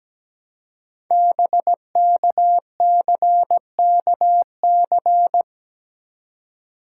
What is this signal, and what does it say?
Morse code, heard as a keyed tone, about 17 wpm, 700 Hz. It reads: BKCKC
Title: Code Groups - 5 Letters - 17wpm